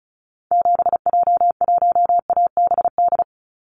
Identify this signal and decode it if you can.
Morse code, heard as a keyed tone, about 36 wpm, 700 Hz. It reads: 7J1ABD
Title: Call Signs - 35wpm